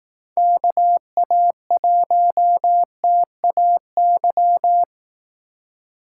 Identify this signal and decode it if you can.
Morse code, heard as a keyed tone, about 18 wpm, 700 Hz. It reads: KA1TAY
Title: Speed Racing - USA 2x3 Call Signs 15wpm